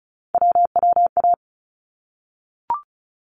Morse code, heard as a keyed tone, about 34 wpm, 700 Hz. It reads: WWA
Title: ARRL Sections 35wpm